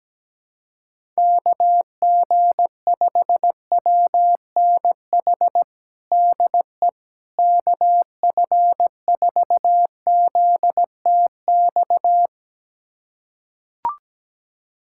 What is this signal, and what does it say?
Morse code, heard as a keyed tone, about 17 wpm, 700 Hz. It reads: KG5WNH DE KF4ZTX
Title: USA Callsign DE Callsign 17wpm